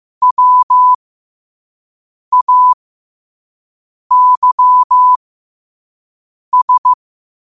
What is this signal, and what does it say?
Morse code, heard as a keyed tone, about 15 wpm, spaced as for 5 wpm, 1000 Hz. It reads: WAYS